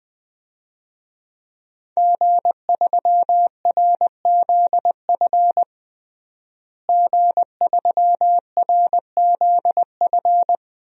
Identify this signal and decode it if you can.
Morse code, heard as a keyed tone, about 21 wpm, 700 Hz. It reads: G3RZF G3RZF
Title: DXCC-100 Call Signs - 5 Characters - Repeated 2x - 20wpm